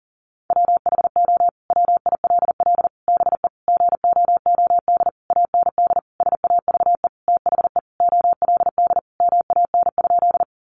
Morse code, heard as a keyed tone, about 40 wpm, 700 Hz. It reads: WHO WILL BE GOOD AND SAVE THE OLD MAN?
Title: Sentences from Top 1100 Words 40wpm